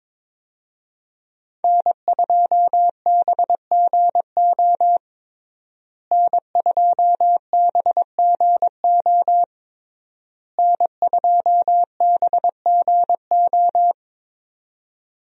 Morse code, heard as a keyed tone, about 22 wpm, 700 Hz. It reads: N2BGO N2BGO N2BGO
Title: USA Call Signs - 5 Characters - Repeated 3x - 22wpm